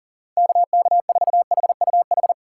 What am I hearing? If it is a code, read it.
Morse code, heard as a keyed tone, about 40 wpm, 700 Hz. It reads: KK4HUH
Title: USA Call Signs - 6 Characters - 2x Character Spacing - 40wpm